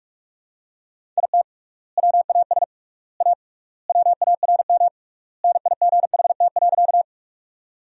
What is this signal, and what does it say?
Morse code, heard as a keyed tone, about 47 wpm, 700 Hz. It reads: IT WAS A WARM NIGHT.
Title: 3x Word-Spacing - Sentences from Top 500 Words - 45wpm